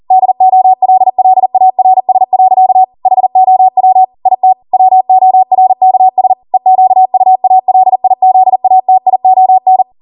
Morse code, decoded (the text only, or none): DOLLARS. HOW IT WORKS EQUALIZATION